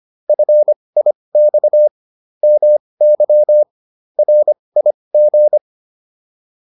Morse code, heard as a keyed tone, about 25 wpm, spaced as for 20 wpm, 600 Hz. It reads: FIX MY RIG